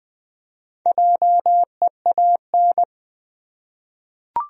Morse code, heard as a keyed tone, about 20 wpm, 700 Hz. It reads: JEAN
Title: USA Common Names - Max 5 Letters - 20wpm